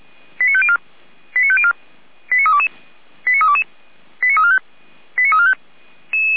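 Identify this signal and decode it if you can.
Five-tone selective call (ZVEI), heard as quick runs of five tones, tones of about 70 ms. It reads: 89504 89504 89322 89322 89345 89345
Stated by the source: Five-tone ZVEI selective call received by SDR, from the Feuerwehr Orlamünde (feuerwehr-orlamuende.org) archive.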